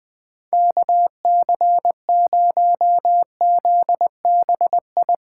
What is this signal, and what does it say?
Morse code, heard as a keyed tone, about 21 wpm, 700 Hz. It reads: KC0ZBI